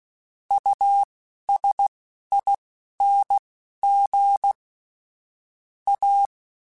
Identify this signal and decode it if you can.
Morse code, heard as a keyed tone, about 16 wpm, spaced as for 11 wpm, 800 Hz. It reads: USING A
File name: Technical Subjects / Technical Subjects-mp3s / Gates Intro @10